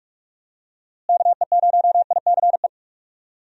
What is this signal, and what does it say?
Morse code, heard as a keyed tone, about 46 wpm, 700 Hz. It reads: KE0ICE
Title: USA Call Signs - 6 Characters - Repeated 3x - 45wpm